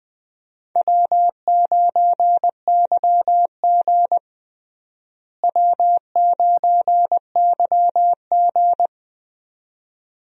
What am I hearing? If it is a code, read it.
Morse code, heard as a keyed tone, about 20 wpm, 700 Hz. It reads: W9YG W9YG
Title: USA Call Signs - 4 Characters - Repeated 2x - 20wpm